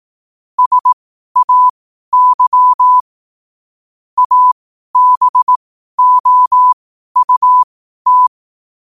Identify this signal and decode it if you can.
Morse code, heard as a keyed tone, about 18 wpm, spaced as for 12 wpm, 1000 Hz. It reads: SAY ABOUT
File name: qotdfiles / 12wpm